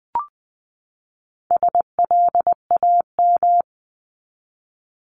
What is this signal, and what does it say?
Morse code, heard as a keyed tone, about 20 wpm, 700 Hz. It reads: SLAM
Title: 260 Lesson 42e - Words 20-20wpm